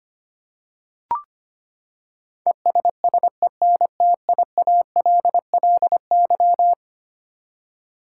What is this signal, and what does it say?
Morse code, heard as a keyed tone, about 25 wpm, 700 Hz. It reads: ESSENTIALLY
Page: morse-code-ninja-practice-25wpm